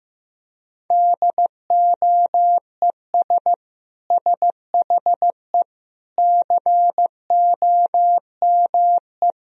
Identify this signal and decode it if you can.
Morse code, heard as a keyed tone, about 16 wpm, 700 Hz. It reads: DOES SHE COME
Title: Sentences from Top 1000 Words - Review Part 4 - 15wpm